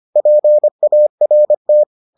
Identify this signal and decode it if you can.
Morse code, heard as a keyed tone, about 25 wpm, 600 Hz. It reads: PART